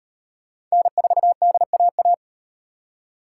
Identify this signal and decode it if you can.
Morse code, heard as a keyed tone, about 37 wpm, 700 Hz. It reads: N4DAA